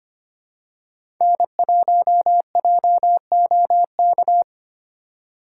Morse code, heard as a keyed tone, about 25 wpm, 700 Hz. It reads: N1JOK
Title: USA Call Signs - 5 Characters - 4x Character Spacing - 25wpm